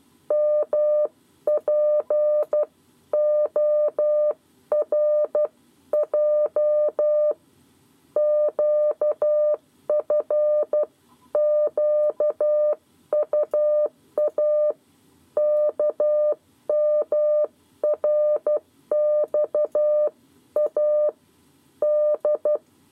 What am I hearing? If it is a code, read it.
Morse code, heard as a keyed tone, about 12 wpm, 600 Hz. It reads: MPORJ QFQUA KMRXA D
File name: 7 Lezione cw 2